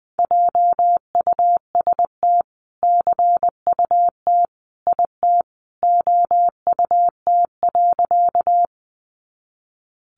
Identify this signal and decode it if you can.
Morse code, heard as a keyed tone, about 20 wpm, 700 Hz. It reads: JUST CUT IT OUT.
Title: Sentences from Top 300 Words 20wpm